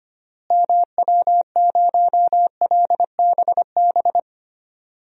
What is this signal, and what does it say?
Morse code, heard as a keyed tone, about 25 wpm, 700 Hz. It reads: MW0LBB
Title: DXCC-100 Call Signs - 6 Characters - Repeated 2x - 25wpm